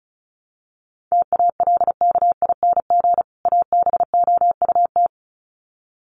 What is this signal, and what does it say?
Morse code, heard as a keyed tone, about 34 wpm, 700 Hz. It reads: TALKING ABOUT
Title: Sets of 2 Words 35wpm